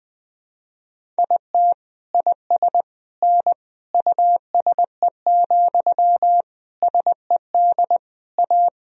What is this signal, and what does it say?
Morse code, heard as a keyed tone, about 20 wpm, 700 Hz. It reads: IT IS N USE, SED A
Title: Sentences from Top 500 Words and Abbreviations 20wpm